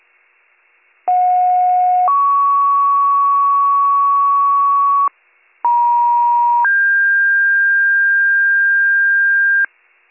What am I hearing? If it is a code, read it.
Two-tone page, heard as a one-second tone, then a three-second tone, about 730 then 1120 Hz, about 950 then 1600 Hz, with noise. It is heard from 1.1 to 9.7 s.